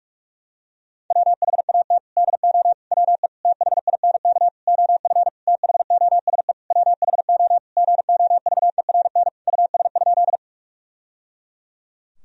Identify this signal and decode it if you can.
Morse code, heard as a keyed tone, about 43 wpm, 700 Hz. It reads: WHAT DO WE THINK OF THOSE WHO GOVERN US?
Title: Sentences from Top 1000 Words - Review Part 2 - 45wpm